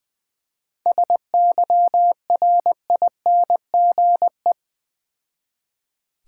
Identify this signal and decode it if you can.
Morse code, heard as a keyed tone, about 20 wpm, 700 Hz. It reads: SYRINGE
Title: Difficult Words with Hints - 20wpm